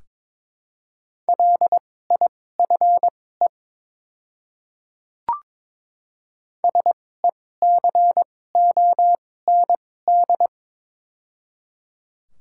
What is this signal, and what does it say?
Morse code, heard as a keyed tone, about 22 wpm, spaced as for 16 wpm, 700 Hz. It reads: LIFE SECOND
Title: 2x Character Spacing - Top 300 Words - 22wpm